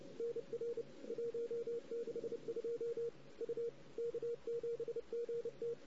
Morse code, heard as a keyed tone, about 27 wpm, 460 Hz, with radio noise. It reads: NR 162 U K7GT
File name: CW / 2008-ss-cw-k5tr-op-n5rz -16:00